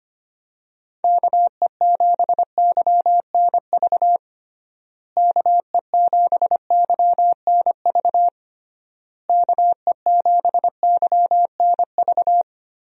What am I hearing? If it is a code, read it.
Morse code, heard as a keyed tone, about 25 wpm, 700 Hz. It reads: KE7YNV KE7YNV KE7YNV
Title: USA Call Signs - 6 Characters - Repeated 3x - 25wpm